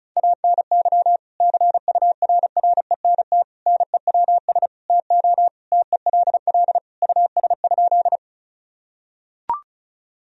Morse code, heard as a keyed tone, about 36 wpm, 700 Hz. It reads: ANY CURRENT NEWS TO TELL US?
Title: Sentences from Top 1000 Words - Review Part 4 - 35wpm